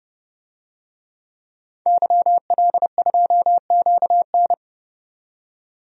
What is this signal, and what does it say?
Morse code, heard as a keyed tone, about 30 wpm, 700 Hz. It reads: YL2QN
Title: DXCC-100 Call Signs - 5 Characters - 30wpm